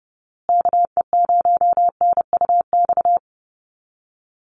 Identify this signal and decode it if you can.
Morse code, heard as a keyed tone, about 30 wpm, 700 Hz. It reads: KE0NUX
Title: USA Call Signs Repeated 3x 30wpm